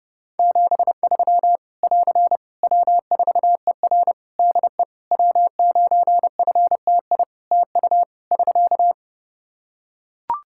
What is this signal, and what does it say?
Morse code, heard as a keyed tone, about 28 wpm, 700 Hz. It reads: 73 <AR> W4ER DE W9FTI TU <SK>
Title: Practice QSO - Exchange 4 - Ending - 30wpm